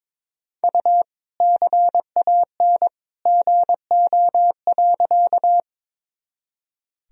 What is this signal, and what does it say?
Morse code, heard as a keyed tone, about 22 wpm, 700 Hz. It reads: U CAN GO.
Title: Sentences from Top 100 Words and Abbreviations 22wpm